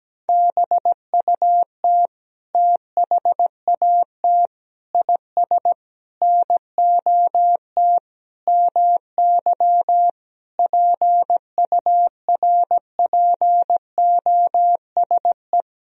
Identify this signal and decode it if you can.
Morse code, heard as a keyed tone, about 17 wpm, 700 Hz. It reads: BUT THAT IS NOT MY PURPOSE
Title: Sentences from Top 1000 Words - Review Part 4 - 17wpm